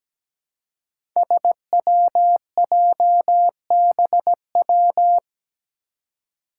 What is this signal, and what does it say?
Morse code, heard as a keyed tone, about 17 wpm, 700 Hz. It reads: SWJBW
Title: Five Challenging Letter-Number Combinations 17wpm